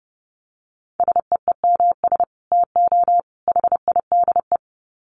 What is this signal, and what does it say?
Morse code, heard as a keyed tone, about 30 wpm, 700 Hz. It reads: SEEMS TO HIDE
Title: Dit Challenge 30wpm